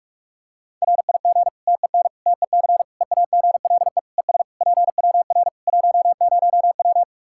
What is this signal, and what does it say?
Morse code, heard as a keyed tone, about 45 wpm, 700 Hz. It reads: RIG TEN TEC EAGLE ES PWR 10W
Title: K1USN SST QSO Practice 45wpm